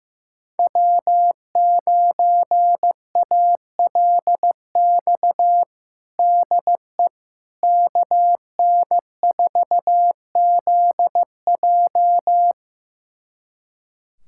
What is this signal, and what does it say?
Morse code, heard as a keyed tone, about 15 wpm, 700 Hz. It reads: W9ALX DE KN4ZJ
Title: USA Callsign DE Callsign - 5 Characters - 15wpm